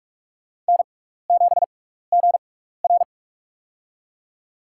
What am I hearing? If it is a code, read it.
Morse code, heard as a keyed tone, about 46 wpm, spaced as for 16 wpm, 700 Hz. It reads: N7GR
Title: USA Call Signs - 4 Characters - 6x Character Spacing - 45wpm